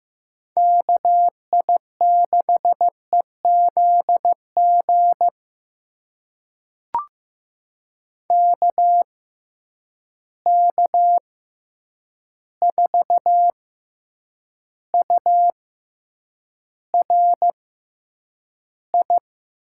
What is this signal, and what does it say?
Morse code, heard as a keyed tone, about 15 wpm, spaced as for 8 wpm, 700 Hz. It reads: KI6EZG KK4URI
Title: USA Call Signs - 6 Characters - 6x Character Spacing - 15wpm